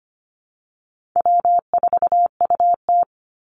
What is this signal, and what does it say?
Morse code, heard as a keyed tone, about 25 wpm, 700 Hz. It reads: W4UT
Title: Call Signs - 25wpm